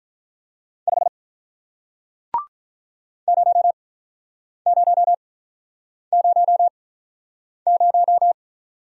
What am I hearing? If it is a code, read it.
Morse code, heard as a keyed tone, about 48 wpm, spaced as for 9 wpm, 700 Hz. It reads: 5 0000